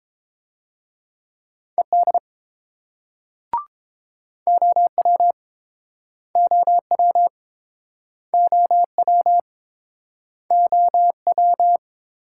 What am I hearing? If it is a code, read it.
Morse code, heard as a keyed tone, about 29 wpm, 700 Hz. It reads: ED OW OW OW OW